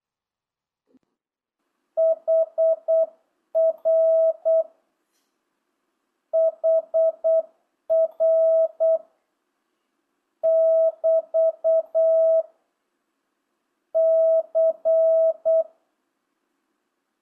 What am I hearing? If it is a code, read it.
Morse code, heard as a keyed tone, about 8 wpm, 650 Hz. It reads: HR HR <BT> C
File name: jh8cbh / 2025cw